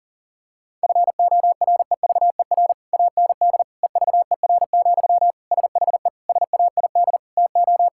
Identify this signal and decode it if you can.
Morse code, heard as a keyed tone, about 40 wpm, 700 Hz. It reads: FOREVER AND EVER, SHE SAID TO
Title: Sentences from Top 1600 Words 40wpm